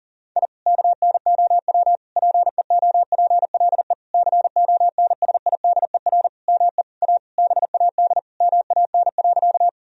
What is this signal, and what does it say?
Morse code, heard as a keyed tone, about 40 wpm, 700 Hz. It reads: I KNOW PEOPLE CONSIDER ME A BAD MAN.
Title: Sentences from Top 1000 Words - Review Part 4 - 40wpm